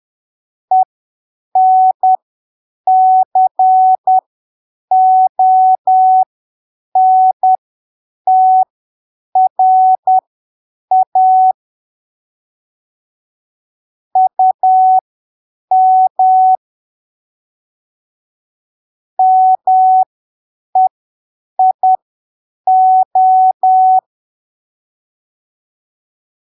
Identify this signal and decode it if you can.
Morse code, heard as a keyed tone, about 10 wpm, spaced as for 7 wpm, 750 Hz. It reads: ENCONTRA UM MEIO
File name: arquivos / texto / texto 23 10ppm